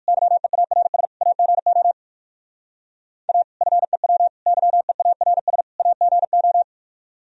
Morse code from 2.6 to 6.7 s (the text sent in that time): A FEW YEARS AGO